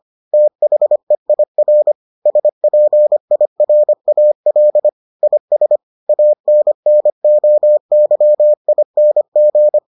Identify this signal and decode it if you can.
Morse code, heard as a keyed tone, about 25 wpm, 600 Hz. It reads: THEIR SPIRAL IS ANNOYING